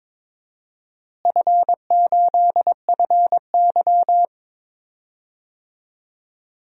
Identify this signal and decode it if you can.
Morse code, heard as a keyed tone, about 21 wpm, 700 Hz. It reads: F8FY
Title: DXCC-100 Call Signs - 4 Characters - Repeated 3x - 22wpm